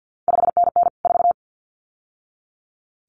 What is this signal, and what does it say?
Morse code, heard as a keyed tone, about 50 wpm, 700 Hz. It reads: 5NN 4
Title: CQ WW DX Contest Exchange - 50wpm